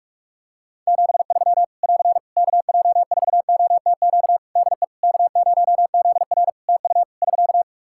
Morse code, heard as a keyed tone, about 46 wpm, 700 Hz. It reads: 73 <AR> KJ4OTQ DE K0ZR TU <SK>
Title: Practice QSO - Exchange 4 - Ending - 45wpm